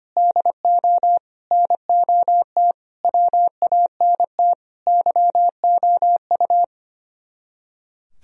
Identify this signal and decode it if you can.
Morse code, heard as a keyed tone, about 31 wpm, 700 Hz. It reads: DO NOT WANT YOU
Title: Speed-Racing - Sets of 5 Words 25wpm